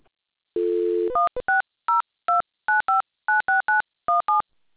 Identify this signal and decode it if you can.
DTMF, heard as telephone touch tones, tones of about 150 ms. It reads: 16029596917